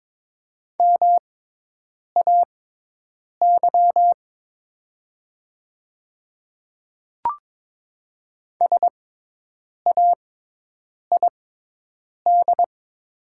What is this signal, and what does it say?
Morse code, heard as a keyed tone, about 23 wpm, spaced as for 8 wpm, 700 Hz. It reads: MAY SAID
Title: 6x Character Spacing - Top 100 Words - 22wpm